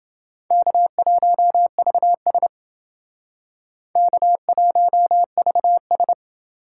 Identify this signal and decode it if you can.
Morse code, heard as a keyed tone, about 28 wpm, 700 Hz. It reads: K1VS K1VS